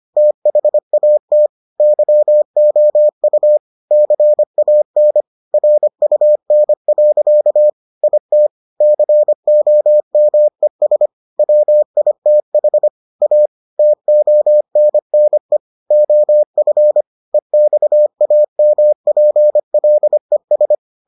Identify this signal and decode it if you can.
Morse code, heard as a keyed tone, about 25 wpm, 600 Hz. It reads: THAT YOU CAN RUN. IT COMES WITH A TONNE OF EXAMPLES